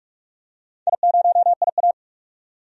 Morse code, heard as a keyed tone, about 43 wpm, 700 Hz. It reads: I0IA
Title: DXCC-100 Call Signs - 4 Characters - 45wpm